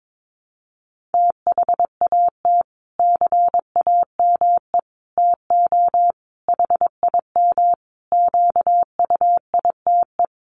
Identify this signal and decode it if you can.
Morse code, heard as a keyed tone, about 22 wpm, 700 Hz. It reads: THAT CAME TO HIM QUITE
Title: Sentences from Top 1300 Words 22wpm